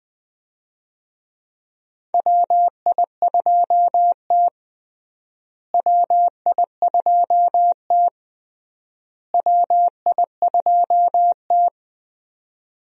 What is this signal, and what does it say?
Morse code, heard as a keyed tone, about 20 wpm, 700 Hz. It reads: WI2T WI2T WI2T
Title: USA Call Signs - 4 Characters - Repeated 3x - 20wpm